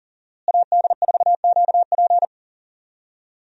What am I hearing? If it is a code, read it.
Morse code, heard as a keyed tone, about 40 wpm, 700 Hz. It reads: AD4QP
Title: USA Call Signs - 5 Characters - Repeated 3x - 40wpm